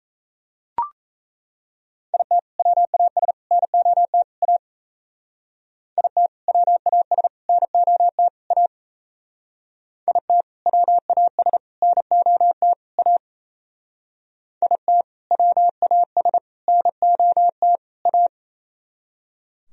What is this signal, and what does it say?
Morse code, heard as a keyed tone, about 33 wpm, 700 Hz. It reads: IT WAS NOT A IT WAS NOT A IT WAS NOT A IT WAS NOT A
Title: Speed-Racing - Sets of 4 Words 28wpm